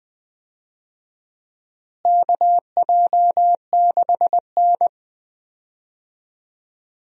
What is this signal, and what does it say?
Morse code, heard as a keyed tone, about 20 wpm, 700 Hz. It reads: KJ6N
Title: USA Call Signs - 4 Characters - Repeated 3x - 20wpm